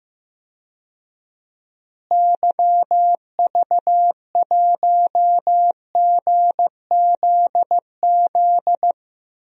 Morse code, heard as a keyed tone, about 16 wpm, 700 Hz. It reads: YV1GZZ
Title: DXCC-100 Call Signs - 6 Characters - Repeated 2x - 15wpm